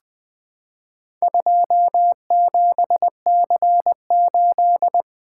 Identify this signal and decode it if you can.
Morse code, heard as a keyed tone, about 20 wpm, 700 Hz. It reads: 27C8
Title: Four Letter-Number Combinations - 20wpm